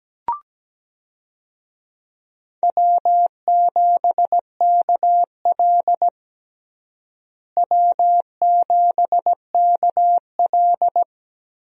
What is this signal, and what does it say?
Morse code, heard as a keyed tone, about 17 wpm, 700 Hz. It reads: W7KL W7KL
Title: USA Call Signs - 4 Characters - Repeated 2x - 17wpm